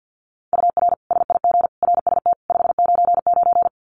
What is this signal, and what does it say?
Morse code, heard as a keyed tone, about 50 wpm, 700 Hz. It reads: UR SIG RST 599
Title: Common QSO Phrases 50wpm